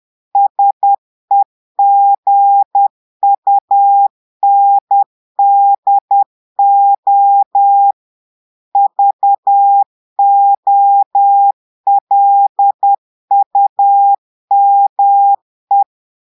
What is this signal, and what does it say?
Morse code, heard as a keyed tone, about 10 wpm, 800 Hz. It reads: SEGUNDO VOLUME
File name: Portugais - texte en clair 1 - vitesse - 10 mots minute